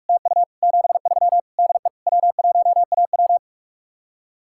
Morse code, heard as a keyed tone, about 44 wpm, 700 Hz. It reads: TU 73 DE W1AW